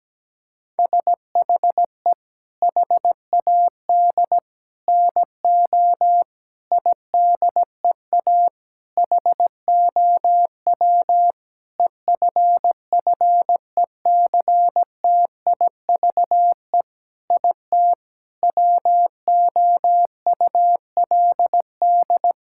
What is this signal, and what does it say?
Morse code, heard as a keyed tone, about 17 wpm, 700 Hz. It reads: SHE HAD NO IDEA HOW EFFECTIVE IT WOULD
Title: Sentences from Top 1000 Words - Review Part 6 - 17wpm